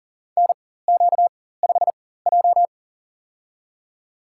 Morse code, heard as a keyed tone, about 40 wpm, spaced as for 19 wpm, 700 Hz. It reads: NQ5J
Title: USA Call Signs - 4x Character Spacing - 40wpm